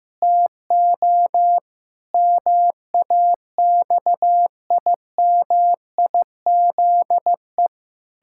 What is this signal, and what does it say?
Morse code, heard as a keyed tone, about 15 wpm, 700 Hz. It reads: TO MAXIMIZE
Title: Pangram Challenge v2 - 15wpm